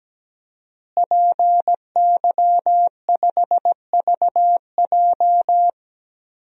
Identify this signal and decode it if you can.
Morse code, heard as a keyed tone, about 17 wpm, 700 Hz. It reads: PY5VJ
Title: DXCC-100 Call Signs - 5 Characters - Repeated 3x - 17wpm